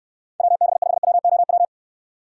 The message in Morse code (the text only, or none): WB5WOW